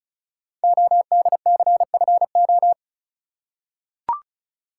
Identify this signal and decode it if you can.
Morse code, heard as a keyed tone, about 36 wpm, 690 Hz. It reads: ODCFO
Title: Code Groups - 5 Letters - 35wpm